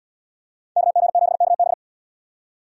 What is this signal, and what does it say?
Morse code, heard as a keyed tone, about 77 wpm, 690 Hz. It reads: KG7KB